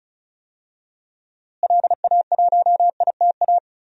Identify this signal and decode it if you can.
Morse code, heard as a keyed tone, about 36 wpm, 700 Hz. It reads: LA1ITA
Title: DXCC-100 Call Signs - 6 Characters - Repeated 3x - 35wpm